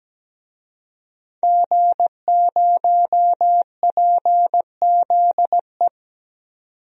Morse code, heard as a keyed tone, about 17 wpm, 700 Hz. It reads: G0PZE
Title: DXCC-100 Call Signs - 5 Characters - 17wpm